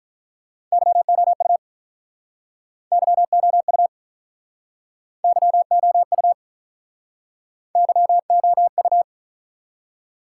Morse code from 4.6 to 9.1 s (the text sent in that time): YOU YOU